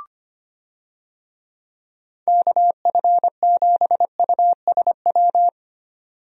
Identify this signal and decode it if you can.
Morse code, heard as a keyed tone, about 25 wpm, 700 Hz. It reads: KF7USW
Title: USA Call Signs - 6 Characters - 25wpm